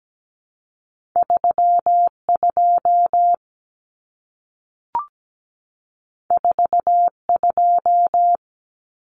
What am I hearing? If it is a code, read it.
Morse code, heard as a keyed tone, about 17 wpm, 700 Hz. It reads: 32 42
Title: Common Cut Numbers - 2-Digit 17wpm